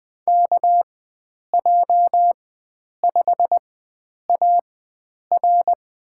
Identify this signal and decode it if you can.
Morse code, heard as a keyed tone, about 20 wpm, spaced as for 9 wpm, 700 Hz. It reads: KJ5AR
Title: USA Call Signs - 5 Characters - 4x Character Spacing - 20wpm